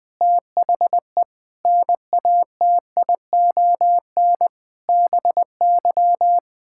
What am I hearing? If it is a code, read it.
Morse code, heard as a keyed tone, about 20 wpm, 700 Hz. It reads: THE NATION BY
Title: Sets of 9 Words - 20wpm